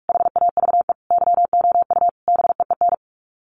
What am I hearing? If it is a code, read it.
Morse code, heard as a keyed tone, about 44 wpm, 700 Hz. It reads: HAVE YOU BEEN
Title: Sets of 5 Words Encore 45wpm